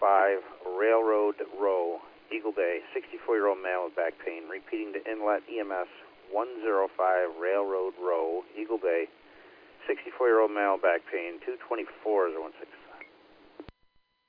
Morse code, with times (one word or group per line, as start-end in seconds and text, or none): none